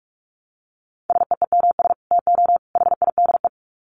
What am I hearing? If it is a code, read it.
Morse code, heard as a keyed tone, about 44 wpm, 700 Hz. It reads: SEEMS TO HIDE
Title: Dit Challenge 45wpm